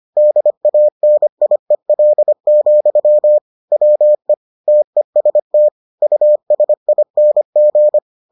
Morse code, heard as a keyed tone, about 25 wpm, 600 Hz. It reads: DANIEL, WE TEST USING